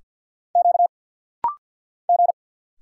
Morse code, heard as a keyed tone, about 50 wpm, 690 Hz. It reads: Q G
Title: Single Letter-Number Alt - Rapid-Fire 50wpm